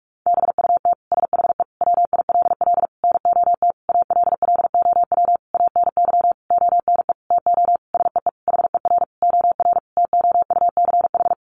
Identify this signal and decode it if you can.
Morse code, heard as a keyed tone, about 44 wpm, 700 Hz. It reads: BUT SHE WILL NOT ALLOW ANY ONE TO SEE HER OR TOUCH
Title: Sentences from Top 1200 Words 45wpm